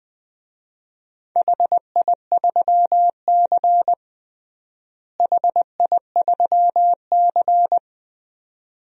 Morse code, heard as a keyed tone, about 20 wpm, 700 Hz. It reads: HI3C HI3C
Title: DXCC-100 Call Signs - 4 Characters - Repeated 2x - 20wpm